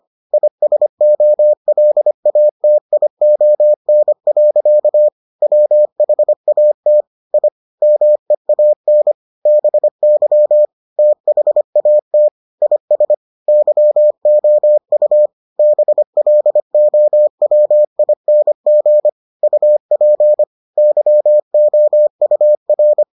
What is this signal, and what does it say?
Morse code, heard as a keyed tone, about 25 wpm, 600 Hz. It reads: ISOLATION. WHAT I MEAN BY THAT IS YOU BLOWING UP YOUR